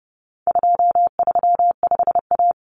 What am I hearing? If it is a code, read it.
Morse code, heard as a keyed tone, about 30 wpm, 700 Hz. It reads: 235A
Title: Common Cut Numbers - 4-Digit 30wpm